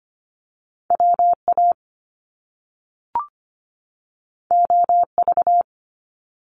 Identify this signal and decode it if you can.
Morse code, heard as a keyed tone, about 25 wpm, 700 Hz. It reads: WA OV
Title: Top Two Letter Combinations in Words 25wpm